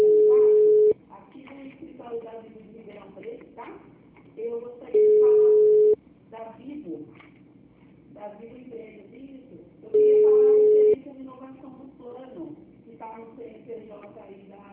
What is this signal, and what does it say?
Ringback tone (Europe, 425 Hz), with speech.
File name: out- -2003- - - .251